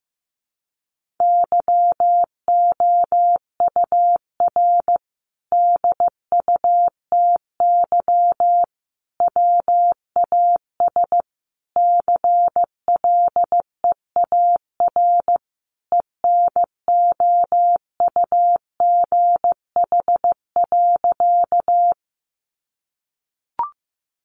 Morse code, heard as a keyed tone, about 15 wpm, 700 Hz. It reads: YOUR DUTY WAS CLEAR ENOUGH.
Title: Sentences from Top 1500 Words 15wpm